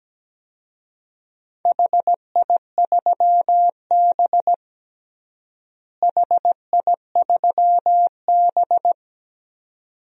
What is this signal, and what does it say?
Morse code, heard as a keyed tone, about 17 wpm, 700 Hz. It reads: HI3B HI3B
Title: DXCC-100 Call Signs - 4 Characters - Repeated 2x - 17wpm